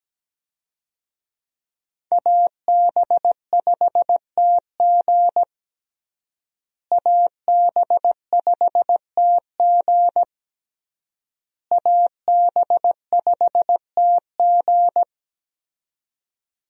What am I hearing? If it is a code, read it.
Morse code, heard as a keyed tone, about 17 wpm, 700 Hz. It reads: AB5TG AB5TG AB5TG
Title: USA Call Signs - 5 Characters - Repeated 3x - 17wpm